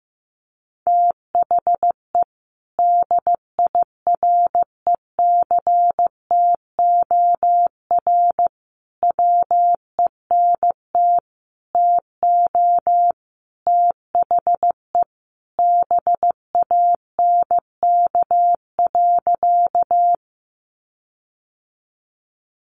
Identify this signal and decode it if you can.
Morse code, heard as a keyed tone, about 15 wpm, 700 Hz. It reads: THE DIRECTOR WENT TO THE BANK.
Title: Sentences from Top 1300 Words 15wpm